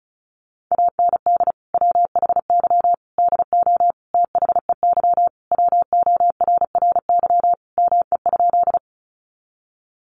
Morse code, heard as a keyed tone, about 34 wpm, 700 Hz. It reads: AND WHY DO THEY WORRY ME?
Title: Sentences from Top 1100 Words 35wpm